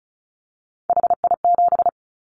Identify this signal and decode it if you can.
Morse code, heard as a keyed tone, about 34 wpm, 700 Hz. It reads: HI7
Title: Callsign Prefixes for Top 100 DXCC - 35wpm